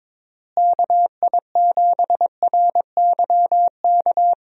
Morse code, heard as a keyed tone, about 22 wpm, 700 Hz. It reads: KI7RYK
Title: USA Call Signs - 6 Characters - Repeated 2x - 22wpm